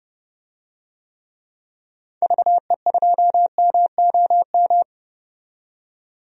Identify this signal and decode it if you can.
Morse code, heard as a keyed tone, about 32 wpm, 700 Hz. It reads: VE2MOM
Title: DXCC-100 Call Signs - 6 Characters - 30wpm